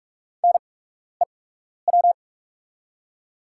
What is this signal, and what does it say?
Morse code, heard as a keyed tone, about 44 wpm, spaced as for 12 wpm, 690 Hz. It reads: NEW
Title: 8x Character Spacing - Top 300 Words - 45wpm